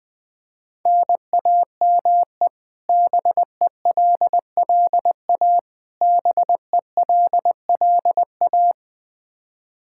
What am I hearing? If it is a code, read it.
Morse code, heard as a keyed tone, about 20 wpm, 700 Hz. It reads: NAME BELLA BELLA
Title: Practice QSO - Exchange 1 - Beginning - 20wpm